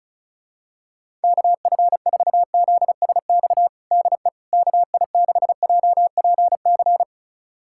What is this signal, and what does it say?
Morse code, heard as a keyed tone, about 36 wpm, 700 Hz. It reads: KF4ZSX DE KI6JPC
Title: USA Callsign DE Callsign 35wpm